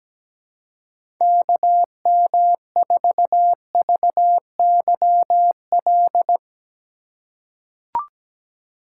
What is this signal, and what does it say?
Morse code, heard as a keyed tone, about 17 wpm, 700 Hz. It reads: KM4VYL
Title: USA Call Signs - 6 Characters - 2x Character Spacing - 17wpm